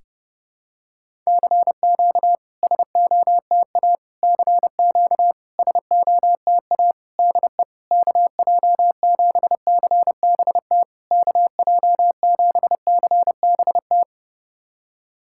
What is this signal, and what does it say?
Morse code, heard as a keyed tone, about 30 wpm, 700 Hz. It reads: CQ SOTA CQ SOTA DE KJ7CBT KJ7CBT
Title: SOTA QSO Practice 30wpm